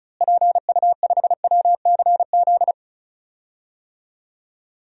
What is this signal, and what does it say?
Morse code, heard as a keyed tone, about 34 wpm, 700 Hz. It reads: PU5WCZ
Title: DXCC-100 Call Signs - 6 Characters - Repeated 3x - 35wpm